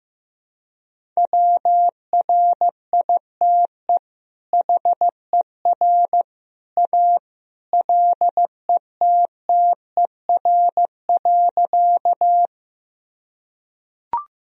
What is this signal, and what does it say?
Morse code, heard as a keyed tone, about 15 wpm, 700 Hz. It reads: WRITE HER A LETTER.
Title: Sentences from Top 1000 Words - Review Part 1 - 15wpm